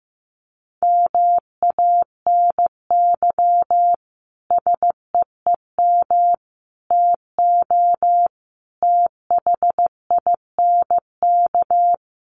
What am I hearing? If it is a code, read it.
Morse code, heard as a keyed tone, about 15 wpm, 700 Hz. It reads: MANY SEEM TO THINK
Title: Sentences from Top 1400 Words 15wpm